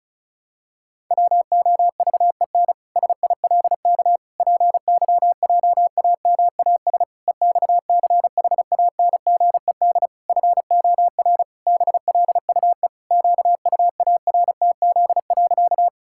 Morse code, heard as a keyed tone, about 36 wpm, 700 Hz. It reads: WOVEN SILK PYJAMAS EXCHANGED FOR BLUE QUARTZ.
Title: Pangram Challenge v2 - 35wpm